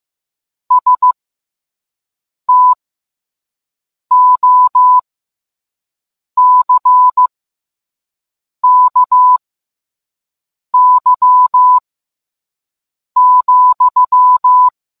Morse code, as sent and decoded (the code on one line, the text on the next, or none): ... - --- -.-. -.- -.-- --..--
STOCKY,